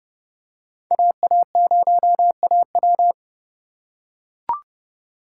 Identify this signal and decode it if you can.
Morse code, heard as a keyed tone, about 31 wpm, 700 Hz. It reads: AA0AW